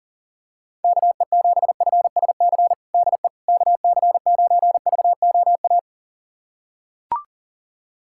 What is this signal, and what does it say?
Morse code, heard as a keyed tone, about 40 wpm, 700 Hz. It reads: KE7FSC DE KC9VOA
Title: USA Callsign DE Callsign - 6 Characters - 40wpm